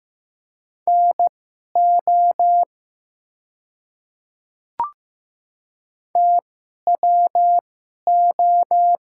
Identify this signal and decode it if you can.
Morse code, heard as a keyed tone, about 15 wpm, spaced as for 11 wpm, 700 Hz. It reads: NO TWO
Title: 2x Character Spacing - Top 200 Words - 15wpm